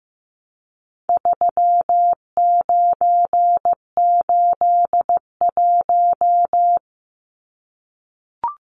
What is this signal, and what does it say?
Morse code, heard as a keyed tone, about 15 wpm, 700 Hz. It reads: 3981
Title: Numbers - 4-Digit 15wpm